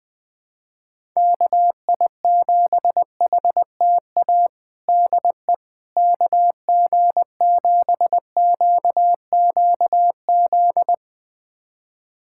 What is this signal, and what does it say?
Morse code, heard as a keyed tone, about 21 wpm, 700 Hz. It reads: KI7HTA DE KG7QQZ
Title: USA Callsign DE Callsign - 6 Characters - 20wpm